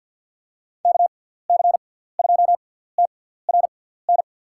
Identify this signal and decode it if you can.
Morse code, heard as a keyed tone, about 52 wpm, spaced as for 17 wpm, 700 Hz. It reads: KC2TRN
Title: USA Call Signs - 6x Character Spacing - 50wpm